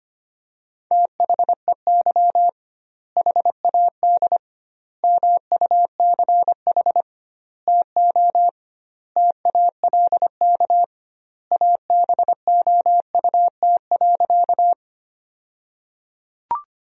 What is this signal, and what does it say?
Morse code, heard as a keyed tone, about 25 wpm, 700 Hz. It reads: THEY HAD MUCH TO TALK ABOUT.
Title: 2x Word-Spacing - Sentences from Top 400 Words - 25wpm